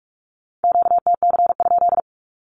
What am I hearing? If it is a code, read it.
Morse code, heard as a keyed tone, about 46 wpm, 700 Hz. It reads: QTC?
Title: Q-Codes with Spoken Meaning - 45wpm